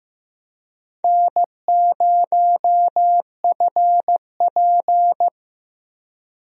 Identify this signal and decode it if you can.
Morse code, heard as a keyed tone, about 15 wpm, 700 Hz. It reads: N0FP
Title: USA Call Signs - 4 Characters - 15wpm